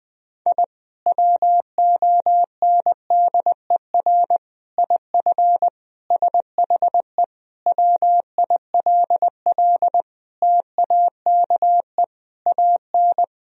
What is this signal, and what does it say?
Morse code, heard as a keyed tone, about 20 wpm, 700 Hz. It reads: I WONDER IF SHE WILL TAKE AN
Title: Sentences from Top 1000 Words - Review Part 3 - 20wpm